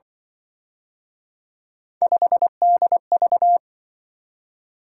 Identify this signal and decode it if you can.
Morse code, heard as a keyed tone, about 25 wpm, 700 Hz. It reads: 5DV
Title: Speed-Racing - Three Letter-Number Combinations - 20wpm